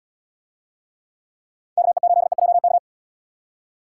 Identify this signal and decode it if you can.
Morse code, heard as a keyed tone, about 70 wpm, 700 Hz. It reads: KE8EJK